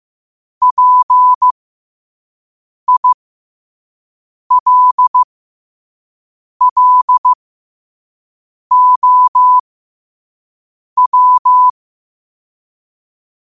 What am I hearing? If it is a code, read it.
Morse code, heard as a keyed tone, about 15 wpm, spaced as for 5 wpm, 1000 Hz. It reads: PILLOW
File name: qotdfiles / 5wpm